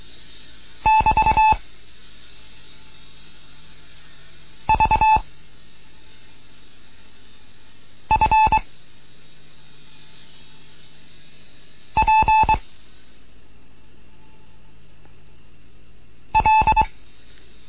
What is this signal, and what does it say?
Morse code, heard as a keyed tone, about 23 wpm, 870 Hz. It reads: <BT> V F P L